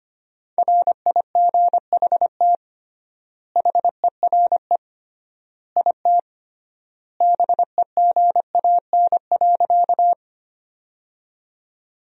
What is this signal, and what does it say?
Morse code, heard as a keyed tone, about 25 wpm, 700 Hz. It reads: RIGHT HERE IT BEGAN.
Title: 3x Word-Spacing - Sentences from Top 300 Words - 25wpm